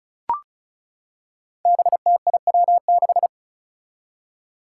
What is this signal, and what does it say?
Morse code, heard as a keyed tone, about 34 wpm, 700 Hz. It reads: BTIW6
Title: Five Challenging Letter-Number Combinations 35wpm